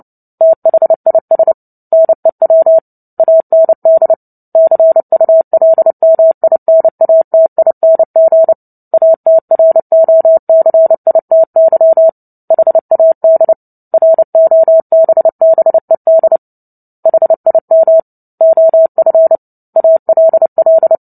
Morse code, heard as a keyed tone, about 30 wpm, 650 Hz. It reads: THIS NEW AND CULMINATING ATROCITY HAD ROBBED HIM OF ALL
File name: Book Two / 03-Chapter 3 - 29 WPM